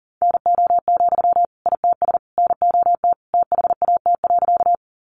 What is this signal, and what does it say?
Morse code, heard as a keyed tone, about 37 wpm, 700 Hz. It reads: NO, ITS NOT THAT.
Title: Sentences from Top 600 Words 40wpm